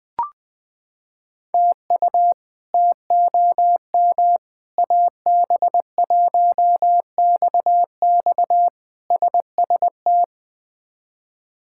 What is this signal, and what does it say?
Morse code, heard as a keyed tone, about 20 wpm, 700 Hz. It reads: TU TOM AB1XX SST
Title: K1USN SST QSO Practice - 20wpm